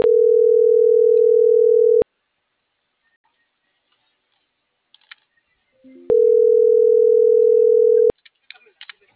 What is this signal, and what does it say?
Ringback tone (North America), 440 + 480 Hz.